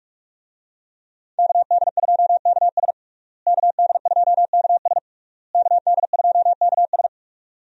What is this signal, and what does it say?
Morse code, heard as a keyed tone, about 46 wpm, 700 Hz. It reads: KD2KS KD2KS KD2KS
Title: USA Call Signs - 5 Characters - Repeated 3x - 45wpm